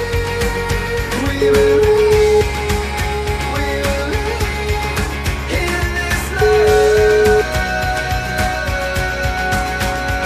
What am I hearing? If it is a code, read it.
Ringback tone (Europe, 425 Hz), with noise.